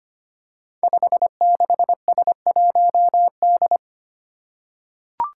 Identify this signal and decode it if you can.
Morse code, heard as a keyed tone, about 25 wpm, 700 Hz. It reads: 56S1D
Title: Five Challenging Letter-Number Combinations 25wpm